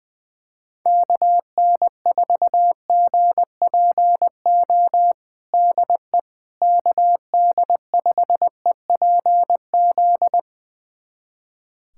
Morse code, heard as a keyed tone, about 20 wpm, 700 Hz. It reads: KN4GPO DE KD5EPZ
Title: USA Callsign DE Callsign - 6 Characters - 20wpm